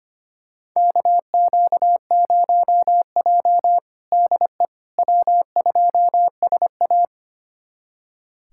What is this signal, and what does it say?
Morse code, heard as a keyed tone, about 25 wpm, 700 Hz. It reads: KQ0J DE W2SA
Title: USA Callsign DE Callsign - 4 Characters - 25wpm